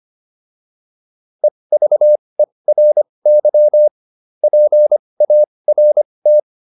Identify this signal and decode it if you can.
Morse code, heard as a keyed tone, about 25 wpm, spaced as for 20 wpm, 600 Hz. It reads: EVERY PART